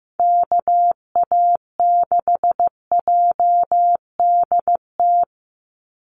Morse code, heard as a keyed tone, about 15 wpm, 700 Hz. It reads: KA6JDT
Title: USA Call Signs 15wpm